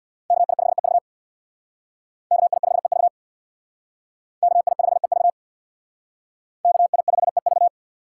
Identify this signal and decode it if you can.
Morse code, heard as a keyed tone, about 60 wpm, 700 Hz. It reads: KI5EV KI5EV KI5EV KI5EV
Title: Speed Racing - USA 2x2 Call Signs 50wpm